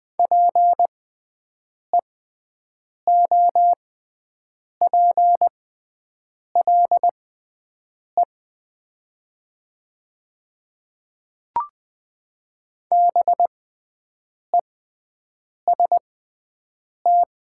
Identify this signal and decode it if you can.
Morse code, heard as a keyed tone, about 20 wpm, spaced as for 7 wpm, 690 Hz. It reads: PEOPLE BEST
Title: 6x Character Spacing - Top 500 Words - 20wpm